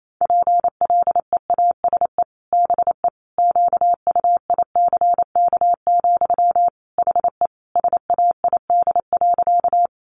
Morse code, heard as a keyed tone, about 28 wpm, 700 Hz. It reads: PLEASE BE QUICK, HE SAID.
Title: Sentences from Top 1500 Words 28wpm